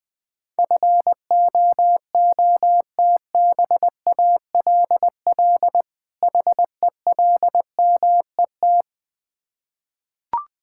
Morse code, heard as a keyed tone, about 20 wpm, 700 Hz. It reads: FOOTBALL HELMET
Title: Difficult Words with Hints - 20wpm